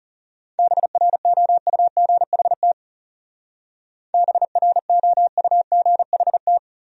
Morse code, heard as a keyed tone, about 38 wpm, 700 Hz. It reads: BROUGHT BROUGHT
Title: Speed-Racing Top 500 Words 30wpm